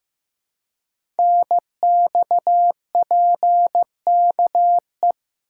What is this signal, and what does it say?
Morse code, heard as a keyed tone, about 15 wpm, 700 Hz. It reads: NXPKE
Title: Code Groups - 5 Letters - 15wpm